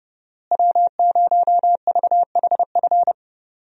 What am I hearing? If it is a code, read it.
Morse code, heard as a keyed tone, about 30 wpm, 700 Hz. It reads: W0VHF
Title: USA Call Signs - 5 Characters - Repeated 3x - 30wpm